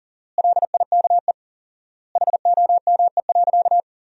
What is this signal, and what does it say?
Morse code, heard as a keyed tone, about 37 wpm, 700 Hz. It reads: LIKE HOME.
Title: 4x Word-Spacing - Sentences from Top 200 Words - 40wpm